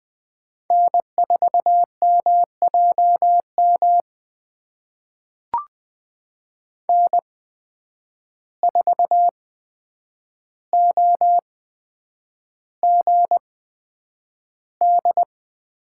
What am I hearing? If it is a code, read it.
Morse code, heard as a keyed tone, about 20 wpm, spaced as for 9 wpm, 700 Hz. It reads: N4MJM N4OGD
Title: USA Call Signs - 5 Characters - 8x Character Spacing - 20wpm